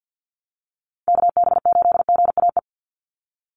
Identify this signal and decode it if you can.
Morse code, heard as a keyed tone, about 50 wpm, 700 Hz. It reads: KB8GAE
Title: Call Signs Repeated 2x - 50wpm